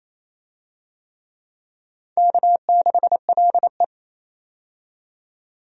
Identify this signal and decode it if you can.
Morse code, heard as a keyed tone, about 28 wpm, 700 Hz. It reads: K6LE
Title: USA Call Signs - 4 Characters - 28wpm